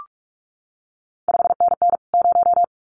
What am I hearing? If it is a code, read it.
Morse code, heard as a keyed tone, about 46 wpm, 700 Hz. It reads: 5NN 0
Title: CQ WW DX Contest Exchange Encore - 45wpm